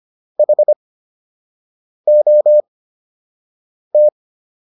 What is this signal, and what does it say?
Morse code, heard as a keyed tone, about 25 wpm, spaced as for 6 wpm, 600 Hz. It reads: HOT